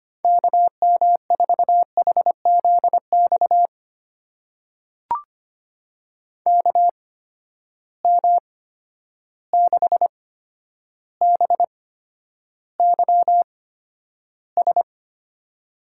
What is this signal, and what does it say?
Morse code, heard as a keyed tone, about 25 wpm, spaced as for 11 wpm, 700 Hz. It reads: KM4HZX KM6BYS
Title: USA Call Signs - 6 Characters - 8x Character Spacing - 25wpm